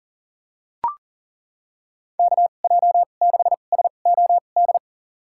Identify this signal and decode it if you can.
Morse code, heard as a keyed tone, about 40 wpm, spaced as for 29 wpm, 700 Hz. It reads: KJ6SOD
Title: USA Call Signs - 6 Characters - 2x Character Spacing - 40wpm